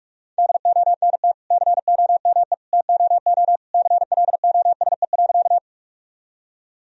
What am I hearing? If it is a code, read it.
Morse code, heard as a keyed tone, about 46 wpm, 700 Hz. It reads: DONT COME TOO CLOSE.
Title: Sentences from Top 1000 Words - Review Part 1 - 45wpm